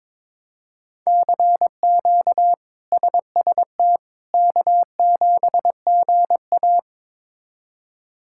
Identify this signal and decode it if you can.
Morse code, heard as a keyed tone, about 22 wpm, 700 Hz. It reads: CQ SST K7GA
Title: K1USN SST QSO Practice - 22wpm